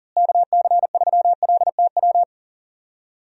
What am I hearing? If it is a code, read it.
Morse code, heard as a keyed tone, about 40 wpm, 700 Hz. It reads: KC3LTW
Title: USA Call Signs - 6 Characters - Repeated 3x - 40wpm